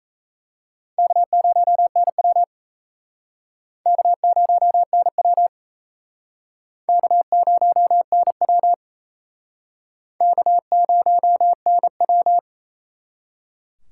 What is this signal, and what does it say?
Morse code, heard as a keyed tone, about 35 wpm, 700 Hz. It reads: K0NW K0NW K0NW K0NW
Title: Speed Racing - USA 1x2 Call Signs 28wpm